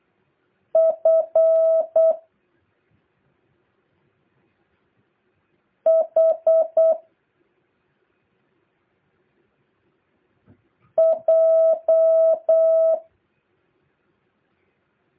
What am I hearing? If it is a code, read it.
Morse code, heard as a keyed tone, about 8 wpm, 650 Hz. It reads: F H J
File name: jh8cbh / 2023cw